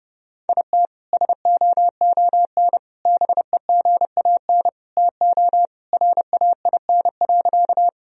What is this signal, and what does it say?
Morse code, heard as a keyed tone, about 32 wpm, 700 Hz. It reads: IT SOON BEGAN TO RAIN.
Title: Sentences from Top 1000 Words - Review Part 1 - 30wpm